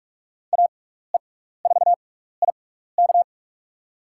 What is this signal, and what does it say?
Morse code, heard as a keyed tone, about 46 wpm, spaced as for 16 wpm, 700 Hz. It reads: AE4IK
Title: USA Call Signs - 6x Character Spacing - 45wpm